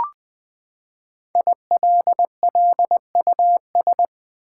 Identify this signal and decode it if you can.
Morse code, heard as a keyed tone, about 20 wpm, 700 Hz. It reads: ILLUS
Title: Build A Word - 5 Letters At A Time - 12-Letter Words - 20wpm